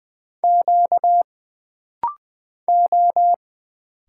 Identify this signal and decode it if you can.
Morse code, heard as a keyed tone, about 20 wpm, 700 Hz. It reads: Q O
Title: Single Letters - Rapid-Fire 20wpm